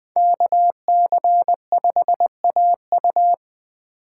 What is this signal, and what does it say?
Morse code, heard as a keyed tone, about 21 wpm, 700 Hz. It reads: KC5AU
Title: USA Call Signs - 5 Characters - 6x Character Spacing - 20wpm